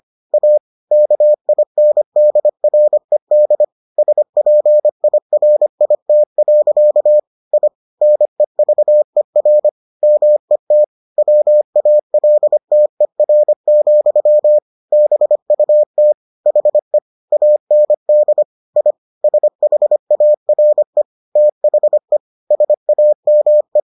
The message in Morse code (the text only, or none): A KINDRED SPIRIT. I NEVER MET WALTER, BUT HE AND I SHARE THE SAME